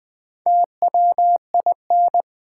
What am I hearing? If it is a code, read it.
Morse code, heard as a keyed tone, about 21 wpm, 700 Hz. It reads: TWIN
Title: Build A Word - 2 Letters At A Time - 4-Letter Words - 20wpm